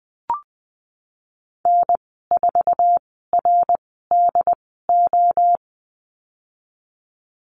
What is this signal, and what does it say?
Morse code, heard as a keyed tone, about 20 wpm, spaced as for 15 wpm, 700 Hz. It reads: N4RDO
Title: 076 Lesson 13g - Letter L - Call Signs 20wpm